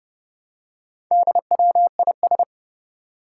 Morse code, heard as a keyed tone, about 29 wpm, 700 Hz. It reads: DWIS
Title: Four Challenging Letter-Number Combinations 30wpm